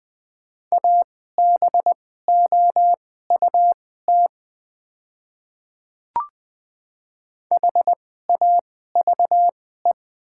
Morse code, heard as a keyed tone, about 20 wpm, spaced as for 14 wpm, 700 Hz. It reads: ABOUT HAVE
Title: 2x Character Spacing - Top 200 Words - 20wpm